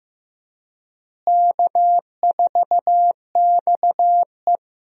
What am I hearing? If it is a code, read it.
Morse code, heard as a keyed tone, about 15 wpm, 700 Hz. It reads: K4XE
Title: USA Call Signs - 4 Characters - Repeated 2x - 15wpm